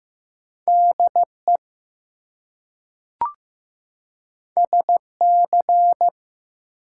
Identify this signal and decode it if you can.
Morse code, heard as a keyed tone, about 15 wpm, 700 Hz. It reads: DE SC
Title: US States and Canadian Provinces 15wpm